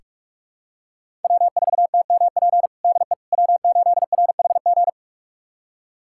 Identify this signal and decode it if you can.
Morse code, heard as a keyed tone, about 46 wpm, 700 Hz. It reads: W4TMP DE W8RHG
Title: USA Callsign DE Callsign - 5 Characters - 45wpm